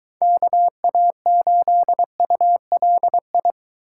Morse code, heard as a keyed tone, about 23 wpm, 700 Hz. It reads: KA8ULI